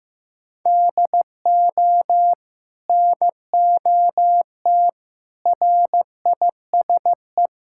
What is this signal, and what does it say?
Morse code, heard as a keyed tone, about 16 wpm, 700 Hz. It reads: DO NOT RISE
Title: Sentences from Top 1000 Words - Review Part 6 - 15wpm